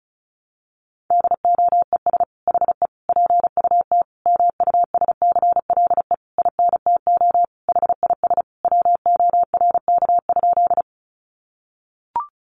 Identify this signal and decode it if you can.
Morse code, heard as a keyed tone, about 34 wpm, 700 Hz. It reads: DOES HE PUT MUSCLE INTO HIS WORK?
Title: Sentences from Top 1300 Words 35wpm